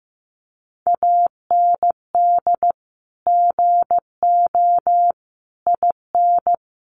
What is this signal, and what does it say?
Morse code, heard as a keyed tone, about 15 wpm, 700 Hz. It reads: AND GO IN
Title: Sentences from Top 1500 Words 15wpm